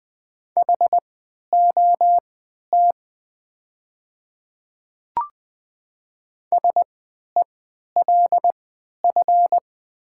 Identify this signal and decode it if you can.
Morse code, heard as a keyed tone, about 20 wpm, spaced as for 11 wpm, 700 Hz. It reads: HOT SELF
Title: 3x Character Spacing - Top 200 Words - 20wpm